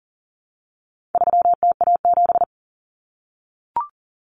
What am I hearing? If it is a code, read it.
Morse code, heard as a keyed tone, about 40 wpm, 700 Hz. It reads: 3TA7
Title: Common Cut Numbers - 4-Digit 40wpm